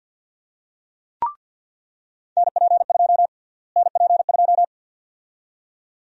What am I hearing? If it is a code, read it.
Morse code, heard as a keyed tone, about 46 wpm, 700 Hz. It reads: NP2 NP2
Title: USA Call Sign Prefixes - Repeated 2x - 50wpm